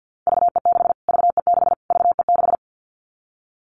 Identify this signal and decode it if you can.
Morse code, heard as a keyed tone, about 50 wpm, 700 Hz. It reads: VE6 VE6 VE6
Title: Callsign Prefixes for Top 25 DXCC - Repeated 3x - 50wpm